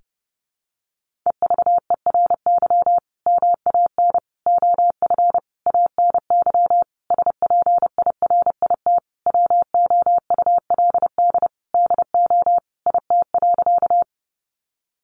Morse code, heard as a keyed tone, about 30 wpm, 700 Hz. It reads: EVERY MAN OF ANY SPIRIT WOULD DO IT.
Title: Sentences from Top 1400 Words 30wpm